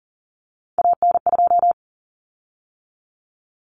Morse code, heard as a keyed tone, about 40 wpm, 700 Hz. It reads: AN2
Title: Common Cut Numbers - 3-Digit 40wpm